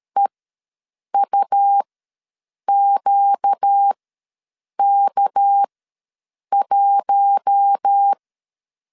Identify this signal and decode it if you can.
Morse code, heard as a keyed tone, about 13 wpm, spaced as for 7 wpm, 800 Hz. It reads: EUQK1
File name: LEZIONI / 8 YJ45 / 8 YJ ws 12wpm 3342